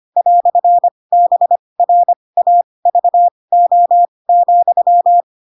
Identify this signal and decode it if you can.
Morse code, heard as a keyed tone, about 25 wpm, spaced as for 20 wpm, 700 Hz. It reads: "BRAVO,
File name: Pinocchio / Pinocchio ch2 15 25 ik1hge